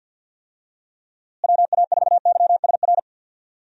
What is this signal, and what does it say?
Morse code, heard as a keyed tone, about 48 wpm, 700 Hz. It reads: WA4YSR
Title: USA Call Signs - 6 Characters - Repeated 3x - 50wpm